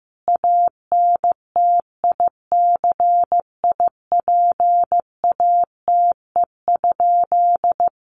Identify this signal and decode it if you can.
Morse code, heard as a keyed tone, about 15 wpm, 700 Hz. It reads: ANTICIPATE?
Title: Sentences from Top 1500 Words 15wpm